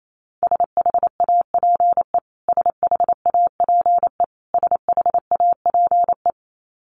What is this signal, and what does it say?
Morse code, heard as a keyed tone, about 28 wpm, 700 Hz. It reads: SHAPE SHAPE SHAPE
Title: Top 500 Words Repeated 28wpm